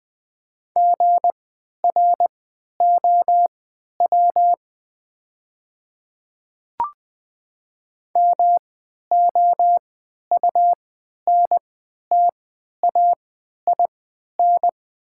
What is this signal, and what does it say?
Morse code, heard as a keyed tone, about 20 wpm, spaced as for 11 wpm, 700 Hz. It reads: GROW MOUNTAIN
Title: 3x Character Spacing - Top 300 Words - 20wpm